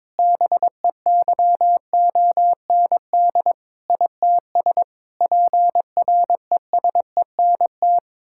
Morse code, heard as a keyed tone, about 22 wpm, 700 Hz. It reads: BEYOND ITS PRESENT
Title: Sentences from Top 1000 Words - Review Part 6 - 22wpm